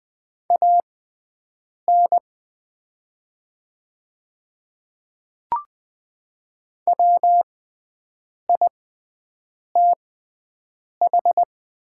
Morse code, heard as a keyed tone, about 20 wpm, spaced as for 7 wpm, 700 Hz. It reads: AN WITH
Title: 6x Character Spacing - Top 100 Words - 20wpm